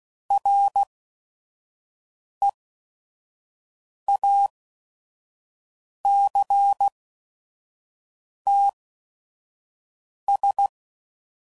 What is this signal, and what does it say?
Morse code, heard as a keyed tone, about 16 wpm, spaced as for 5 wpm, 800 Hz. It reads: REACTS
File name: CW Code Groups @5WPM / Technical Subjects / XL